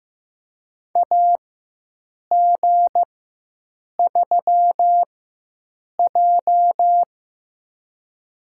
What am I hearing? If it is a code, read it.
Morse code, heard as a keyed tone, about 16 wpm, spaced as for 7 wpm, 700 Hz. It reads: AG3J
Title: USA Call Signs - 4 Characters - 4x Character Spacing - 15wpm